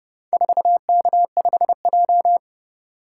Morse code, heard as a keyed tone, about 29 wpm, 700 Hz. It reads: 4K5J